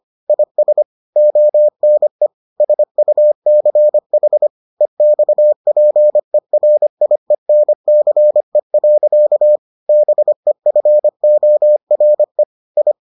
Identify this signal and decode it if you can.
Morse code, heard as a keyed tone, about 25 wpm, 600 Hz. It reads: IS ONE SUCH EXPERIENCE. BEFORE I